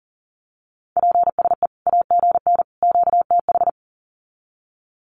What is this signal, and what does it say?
Morse code, heard as a keyed tone, about 40 wpm, 700 Hz. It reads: PSE AGN QTH
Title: Common QSO Phrases 40wpm